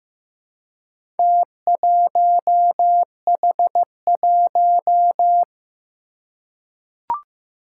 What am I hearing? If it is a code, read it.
Morse code, heard as a keyed tone, about 15 wpm, 700 Hz. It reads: T1H1
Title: Four Challenging Letter-Number Combinations 15wpm